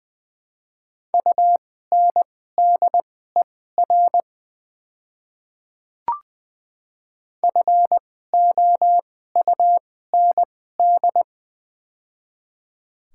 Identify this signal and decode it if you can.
Morse code, heard as a keyed tone, about 20 wpm, spaced as for 14 wpm, 700 Hz. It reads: UNDER FOUND
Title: 2x Character Spacing - Top 300 Words - 20wpm